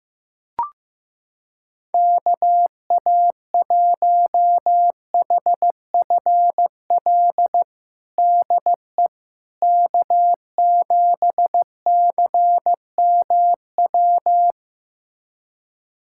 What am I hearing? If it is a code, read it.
Morse code, heard as a keyed tone, about 15 wpm, 700 Hz. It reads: KA1HFL DE K7CMW
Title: USA Callsign DE Callsign 15wpm